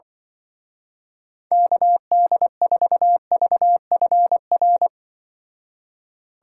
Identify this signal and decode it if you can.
Morse code, heard as a keyed tone, about 25 wpm, 700 Hz. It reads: KD4VFR
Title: Speed Racing - USA 2x3 Call Signs 20wpm